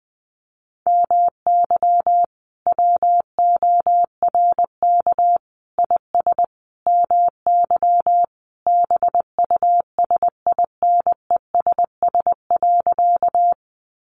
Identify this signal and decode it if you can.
Morse code, heard as a keyed tone, about 20 wpm, 700 Hz. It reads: MY WORK IS MY BUSINESS.
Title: Sentences from Top 600 Words 20wpm